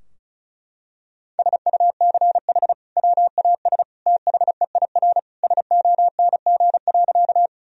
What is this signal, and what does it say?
Morse code, heard as a keyed tone, about 36 wpm, 700 Hz. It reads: SUCH WAS THEIR SONG.
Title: Sentences from Top 1000 Words - Review Part 2 - 35wpm